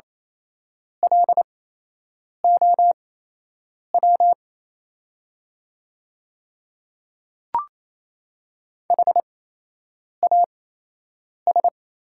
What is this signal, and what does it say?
Morse code, heard as a keyed tone, about 28 wpm, spaced as for 8 wpm, 700 Hz. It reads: LOW HAS
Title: 8x Character Spacing - Top 200 Words - 28wpm